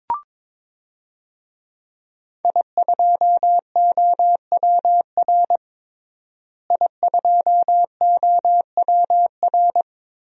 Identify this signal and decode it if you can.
Morse code, heard as a keyed tone, about 22 wpm, 700 Hz. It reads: I2OWR I2OWR
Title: DXCC-100 Call Signs - 5 Characters - Repeated 2x - 22wpm